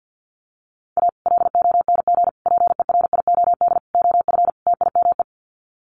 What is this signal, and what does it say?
Morse code, heard as a keyed tone, about 50 wpm, 700 Hz. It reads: A LONG PERIOD OF TIME
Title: Sets of 5 Words 50wpm